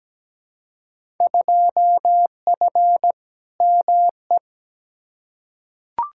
Morse code, heard as a keyed tone, about 17 wpm, 700 Hz. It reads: 2F ME
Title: Field Day Exchanges - 17wpm